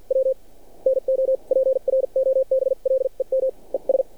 Morse code, heard as a keyed tone, about 49 wpm, spaced as for 41 wpm, 530 Hz. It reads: W NO PROBLEM ES
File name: archived / CRSnet